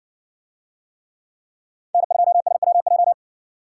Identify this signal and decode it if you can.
Morse code, heard as a keyed tone, about 57 wpm, 700 Hz. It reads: N2SWJ